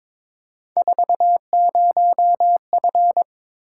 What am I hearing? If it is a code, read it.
Morse code, heard as a keyed tone, about 22 wpm, 700 Hz. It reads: 40F
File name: practice / 22wpm / Three Letter-Number Combinations - 22wpm